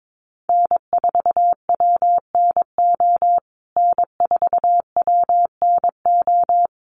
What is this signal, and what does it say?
Morse code, heard as a keyed tone, about 22 wpm, 700 Hz. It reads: N4WNO N4WNO
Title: USA Call Signs Repeated 2x 22wpm